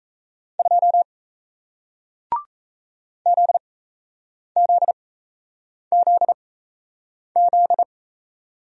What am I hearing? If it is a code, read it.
Morse code, heard as a keyed tone, about 38 wpm, spaced as for 8 wpm, 700 Hz. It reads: 2 ZZZZ